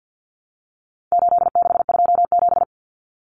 Morse code, heard as a keyed tone, about 50 wpm, 700 Hz. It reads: 8627
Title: Numbers - 4-Digit 50wpm